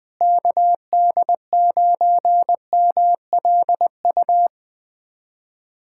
Morse code, heard as a keyed tone, about 20 wpm, 700 Hz. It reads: KD9MLU
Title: USA Callsign DE Callsign - 6 Characters - 20wpm